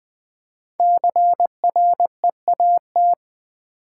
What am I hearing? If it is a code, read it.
Morse code, heard as a keyed tone, about 20 wpm, 700 Hz. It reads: CREAT
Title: Build A Word - 5 Letters At A Time - 10-Letter Words - 20wpm